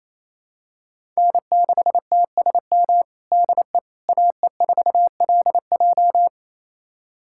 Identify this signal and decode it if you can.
Morse code, heard as a keyed tone, about 27 wpm, 700 Hz. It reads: N6TSM DE AE4LJ
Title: USA Callsign DE Callsign - 5 Characters - 28wpm